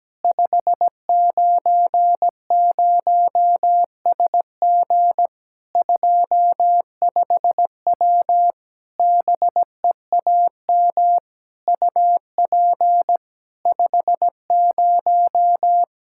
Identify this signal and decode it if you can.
Morse code, heard as a keyed tone, about 17 wpm, 700 Hz. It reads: 590SG 25W BEAM UP 50
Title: Practice QSO - Exchange 2 - Info - 17wpm